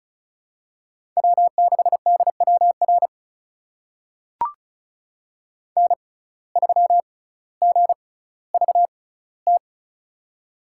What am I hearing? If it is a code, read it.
Morse code, heard as a keyed tone, about 36 wpm, spaced as for 18 wpm, 700 Hz. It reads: W6DWR N3GVT
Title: USA Call Signs - 5 Characters - 6x Character Spacing - 35wpm